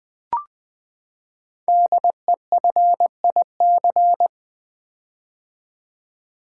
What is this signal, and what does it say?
Morse code, heard as a keyed tone, about 20 wpm, 700 Hz. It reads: DEFIC
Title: Build A Word - 5 Letters At A Time - 10-Letter Words - 20wpm